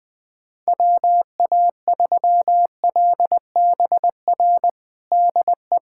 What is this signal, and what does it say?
Morse code, heard as a keyed tone, about 20 wpm, 700 Hz. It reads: WA3LBR DE
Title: USA Callsign DE Callsign - 6 Characters - 20wpm